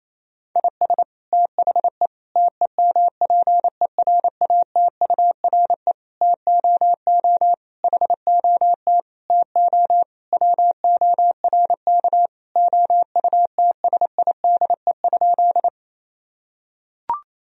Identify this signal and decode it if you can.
Morse code, heard as a keyed tone, about 27 wpm, 700 Hz. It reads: IS THE TEMPERATURE TOO HOT TO WORK OUTSIDE?
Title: Sentences from Top 1000 Words - Review Part 4 - 28wpm